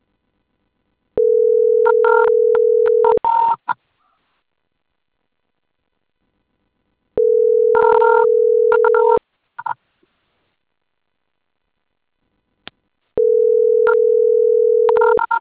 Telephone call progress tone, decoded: ringback tone